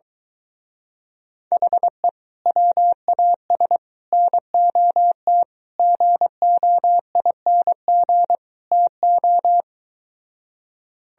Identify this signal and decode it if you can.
Morse code, heard as a keyed tone, about 23 wpm, 700 Hz. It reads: HE WAS NOT GOING TO